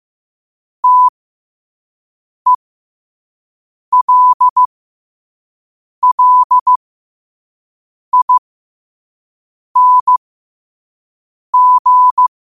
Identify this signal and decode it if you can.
Morse code, heard as a keyed tone, about 15 wpm, spaced as for 5 wpm, 1000 Hz. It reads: TELLING
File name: 5wpm